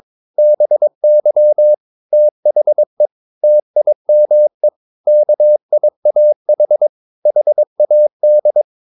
Morse code, heard as a keyed tone, about 23 wpm, 600 Hz. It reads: BY THE TIME KIAH HAD
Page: Romanen Never af Ken Follet på engelsk.